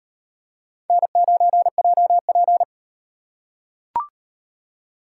Morse code, heard as a keyed tone, about 36 wpm, 700 Hz. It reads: N9JP